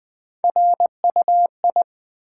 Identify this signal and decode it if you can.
Morse code, heard as a keyed tone, about 20 wpm, 700 Hz. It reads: RUI
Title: Build A Word - 1 Letter At A Time - 3 to 5-Letter Words - 20wpm